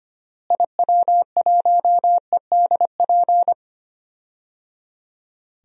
Morse code, heard as a keyed tone, about 25 wpm, 700 Hz. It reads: IW1EDP
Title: DXCC-100 Call Signs - 6 Characters - Repeated 3x - 25wpm